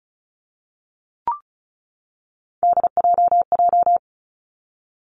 Morse code, heard as a keyed tone, about 36 wpm, 700 Hz. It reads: DJJ
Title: Three Challenging Letter-Number Combinations 35wpm